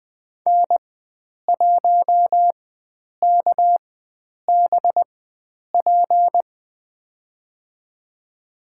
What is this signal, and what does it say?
Morse code, heard as a keyed tone, about 20 wpm, spaced as for 9 wpm, 700 Hz. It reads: N1KBP
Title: USA Call Signs - 5 Characters - 4x Character Spacing - 20wpm